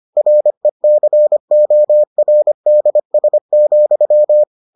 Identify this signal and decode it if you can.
Morse code, heard as a keyed tone, about 25 wpm, 600 Hz. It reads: RECORDS,